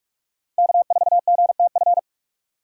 Morse code, heard as a keyed tone, about 46 wpm, 700 Hz. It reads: K4GTF
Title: USA Call Signs - 3x Character Spacing - 45wpm